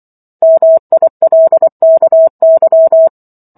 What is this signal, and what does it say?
Morse code, heard as a keyed tone, about 24 wpm, 650 Hz. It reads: MILKY